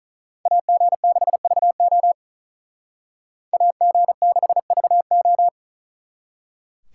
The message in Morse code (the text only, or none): AG6VO AG6VO